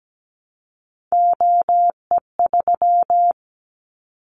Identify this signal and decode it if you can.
Morse code, heard as a keyed tone, about 17 wpm, 700 Hz. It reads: OE3
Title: Callsign Prefixes for Top 25 DXCC - 17wpm